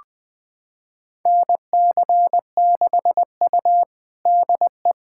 Morse code, heard as a keyed tone, about 21 wpm, 700 Hz. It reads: NC6U DE
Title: USA Callsign DE Callsign - 4 Characters - 20wpm